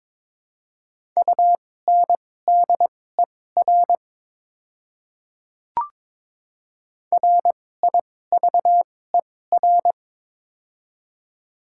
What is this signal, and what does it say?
Morse code, heard as a keyed tone, about 22 wpm, spaced as for 16 wpm, 690 Hz. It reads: UNDER RIVER
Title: 2x Character Spacing - Top 500 Words - 22wpm